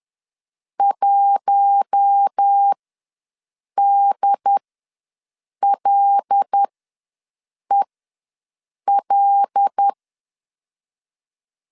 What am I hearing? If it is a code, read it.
Morse code, heard as a keyed tone, about 10 wpm, spaced as for 6 wpm, 800 Hz. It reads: 1DLEL
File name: LEZIONI / 5 LBVC / 5 LBV ws 10wpm 3645